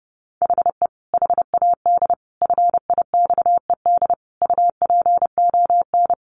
Morse code, heard as a keyed tone, about 28 wpm, 700 Hz. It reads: HE HAD FIXED UPON
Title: Sentences from Top 1500 Words 30wpm